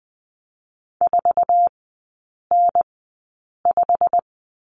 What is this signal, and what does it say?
Morse code, heard as a keyed tone, about 20 wpm, spaced as for 8 wpm, 700 Hz. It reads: 4N5
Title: 087 Lesson 15e - Number 5 - Three Characters 20wpm